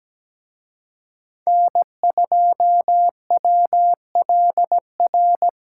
Morse code, heard as a keyed tone, about 20 wpm, 700 Hz. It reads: N2WLR
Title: Speed Racing - USA 1x3 Call Signs 17wpm